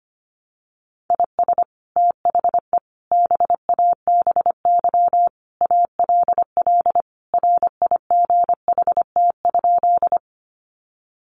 Morse code, heard as a keyed tone, about 25 wpm, 700 Hz. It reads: IS THE BABY ALL RIGHT?
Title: Sentences from Top 1300 Words 25wpm